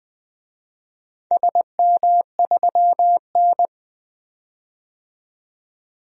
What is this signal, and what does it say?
Morse code, heard as a keyed tone, about 20 wpm, 700 Hz. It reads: SM3N
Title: DXCC-100 Call Signs - 4 Characters - 20wpm